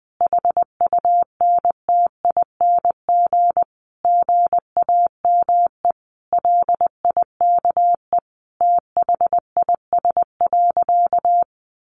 Morse code, heard as a keyed tone, about 20 wpm, 700 Hz. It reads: HUNTING GAME LIKE THIS.
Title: A Sound Of Thunder 20wpm